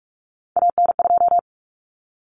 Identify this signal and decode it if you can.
Morse code, heard as a keyed tone, about 44 wpm, 700 Hz. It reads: AN2
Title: Common Cut Numbers - 3-Digit 45wpm